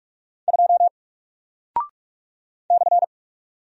Morse code, heard as a keyed tone, about 48 wpm, 700 Hz. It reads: 2 /
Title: Single Character (Letter:Number:Punctuation:Prosign) Alt 45wpm